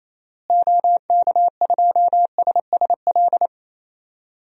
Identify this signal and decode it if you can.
Morse code, heard as a keyed tone, about 28 wpm, 700 Hz. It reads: OK2SSL
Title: DXCC-100 Call Signs - 6 Characters - Repeated 2x - 28wpm